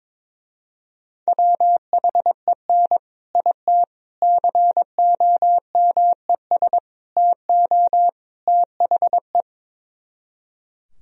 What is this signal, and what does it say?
Morse code, heard as a keyed tone, about 22 wpm, 700 Hz. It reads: WHEN IT COMES TO THE
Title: Speed-Racing - Sets of 5 Words 22wpm